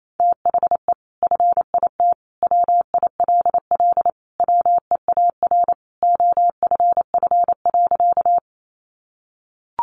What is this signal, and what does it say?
Morse code, heard as a keyed tone, about 28 wpm, 700 Hz. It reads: THE FIT WILL WEAR OFF.
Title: Sentences from Top 1500 Words 28wpm